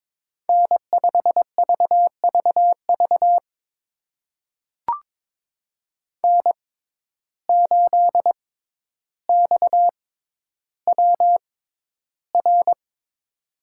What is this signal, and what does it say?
Morse code, heard as a keyed tone, about 22 wpm, spaced as for 11 wpm, 700 Hz. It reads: N5VVV N8XWR
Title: USA Call Signs - 5 Characters - 6x Character Spacing - 22wpm